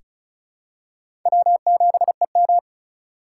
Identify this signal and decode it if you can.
Morse code, heard as a keyed tone, about 36 wpm, 700 Hz. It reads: W7EM
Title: USA Call Signs - 4 Characters - 3x Character Spacing - 35wpm